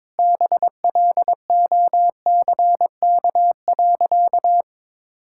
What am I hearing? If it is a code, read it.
Morse code, heard as a keyed tone, about 21 wpm, 700 Hz. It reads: BLOCK.
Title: Sentences from Top 1600 Words 22wpm